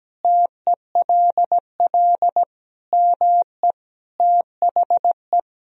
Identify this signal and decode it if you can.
Morse code, heard as a keyed tone, about 17 wpm, 700 Hz. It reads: TELL ME THE
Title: Sentences from Top 1000 Words - Review Part 3 - 17wpm